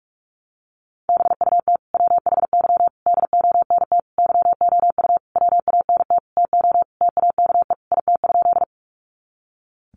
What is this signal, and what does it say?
Morse code, heard as a keyed tone, about 44 wpm, 700 Hz. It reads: BUT WHY DONT YOU WANT TO TAKE IT?
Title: Sentences from Top 300 Words 45wpm